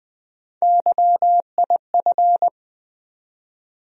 Morse code, heard as a keyed tone, about 20 wpm, 700 Hz. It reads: YIF
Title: USA Call Sign Postfixes - 3 Letters 20wpm